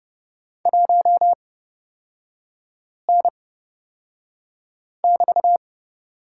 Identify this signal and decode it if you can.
Morse code, heard as a keyed tone, about 30 wpm, spaced as for 5 wpm, 700 Hz. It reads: 1N<BT>
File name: Lesson 36e - BT - Three Characters 30wpm